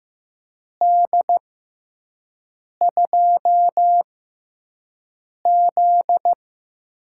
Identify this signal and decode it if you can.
Morse code, heard as a keyed tone, about 15 wpm, spaced as for 5 wpm, 700 Hz. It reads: D2Z
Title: USA Call Signs - 4 Characters - 6x Character Spacing - 15wpm